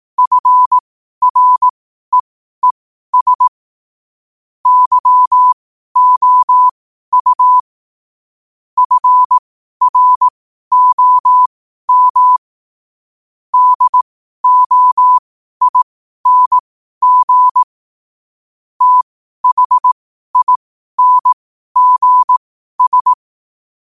Morse code, heard as a keyed tone, about 18 wpm, spaced as for 12 wpm, 1000 Hz. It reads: FREES YOU FROM DOING THINGS
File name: qotdfiles / 12wpm